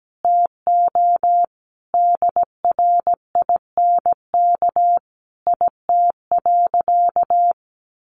Morse code, heard as a keyed tone, about 17 wpm, 700 Hz. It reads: TO DRINK IT.
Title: Additional Easy Phrases 17wpm